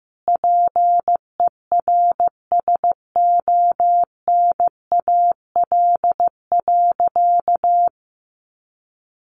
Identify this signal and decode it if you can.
Morse code, heard as a keyed tone, about 15 wpm, 700 Hz. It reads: PERSONAL.
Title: Sentences from Top 700 Words 15wpm